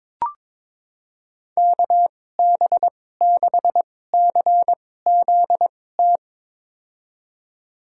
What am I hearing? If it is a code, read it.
Morse code, heard as a keyed tone, about 23 wpm, spaced as for 16 wpm, 700 Hz. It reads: KB6CZT
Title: USA Call Signs - 2x Character Spacing - 22wpm